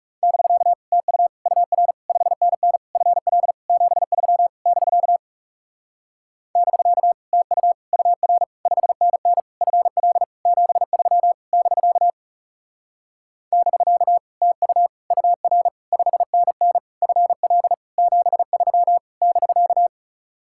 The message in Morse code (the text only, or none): <BK> TU UR 5NN FL 73 <BK> <BK> TU UR 5NN FL 73 <BK> <BK> TU UR 5NN FL 73 <BK>